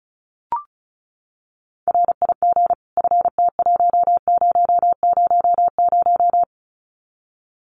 Morse code, heard as a keyed tone, about 34 wpm, 700 Hz. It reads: RIG FT1000
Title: Common QSO Phrases 35wpm